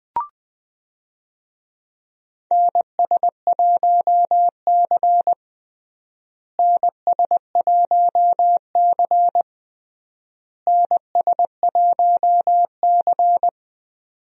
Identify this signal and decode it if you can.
Morse code, heard as a keyed tone, about 20 wpm, 700 Hz. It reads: NS1C NS1C NS1C
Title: USA Call Signs - 4 Characters - Repeated 3x - 20wpm